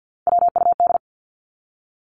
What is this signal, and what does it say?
Morse code, heard as a keyed tone, about 49 wpm, 700 Hz. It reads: WUD
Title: Abbreviations 50wpm